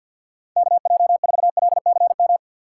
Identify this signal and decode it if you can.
Morse code, heard as a keyed tone, about 52 wpm, 700 Hz. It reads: KJ4LCM
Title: USA Call Signs - 6 Characters - 6x Character Spacing - 50wpm